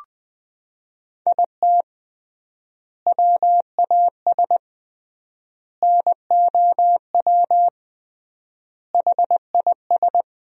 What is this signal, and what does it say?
Morse code, heard as a keyed tone, about 21 wpm, 700 Hz. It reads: IT WAS NOW HIS
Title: 3x Word-Spacing - Sentences from Top 200 Words - 20wpm